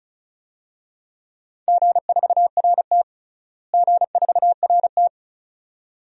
Morse code, heard as a keyed tone, about 34 wpm, 700 Hz. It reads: G4RT G4RT
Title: DXCC-100 Call Signs - 4 Characters - Repeated 2x - 35wpm